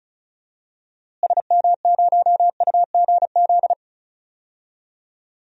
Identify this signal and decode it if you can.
Morse code, heard as a keyed tone, about 36 wpm, 700 Hz. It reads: SM0UGZ
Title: DXCC-100 Call Signs - 6 Characters - Repeated 3x - 35wpm